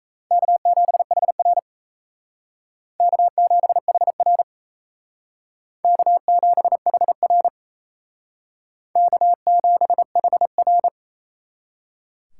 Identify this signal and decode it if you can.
Morse code, heard as a keyed tone, about 35 wpm, 700 Hz. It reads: K7HR K7HR K7HR K7HR
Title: Speed Racing - USA 1x2 Call Signs 28wpm